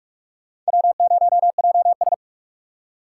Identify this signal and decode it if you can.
Morse code, heard as a keyed tone, about 59 wpm, spaced as for 52 wpm, 700 Hz. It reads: W0JS
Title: Speed Racing - USA 1x2 Call Signs 45wpm